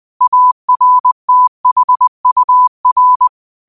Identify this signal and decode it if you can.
Morse code, heard as a keyed tone, about 20 wpm, 1000 Hz. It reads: ARTHUR
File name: qotdfiles / 20wpm